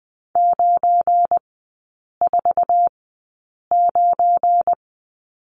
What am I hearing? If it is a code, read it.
Morse code, heard as a keyed tone, about 20 wpm, spaced as for 8 wpm, 700 Hz. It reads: 949
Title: 172 Lesson 28e - Number 9 - Three Characters 20wpm